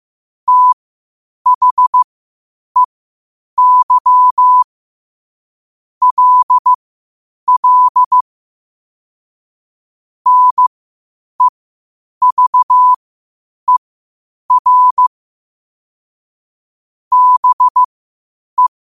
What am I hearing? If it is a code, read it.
Morse code, heard as a keyed tone, about 15 wpm, spaced as for 8 wpm, 1000 Hz. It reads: THEY LL NEVER BE
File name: qotdfiles / 8wpm